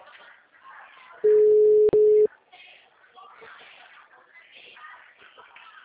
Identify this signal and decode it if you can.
Ringback tone (Europe, 425 Hz), with speech.
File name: out- -2013- - - .10